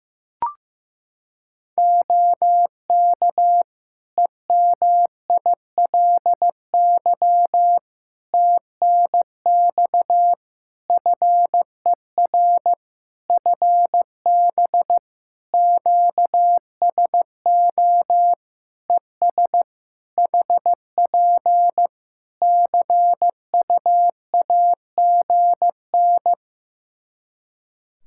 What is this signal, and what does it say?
Morse code, heard as a keyed tone, about 15 wpm, 700 Hz. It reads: OK EMILY TNX FER FB QSO ES HP CUAGN
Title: Practice QSO - Exchange 4 - Ending - 15wpm